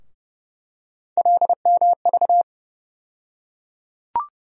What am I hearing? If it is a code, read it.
Morse code, heard as a keyed tone, about 30 wpm, 700 Hz. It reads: LMV
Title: USA Call Sign Postfixes - 3 Letters 30wpm